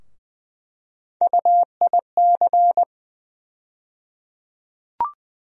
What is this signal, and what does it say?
Morse code, heard as a keyed tone, about 20 wpm, 700 Hz. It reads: UIC
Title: USA Call Sign Postfixes - 3 Letters 20wpm